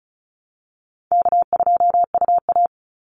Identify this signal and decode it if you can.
Morse code, heard as a keyed tone, about 34 wpm, 700 Hz. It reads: K2UA
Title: Easy Call Signs Repeated 2x - 35wpm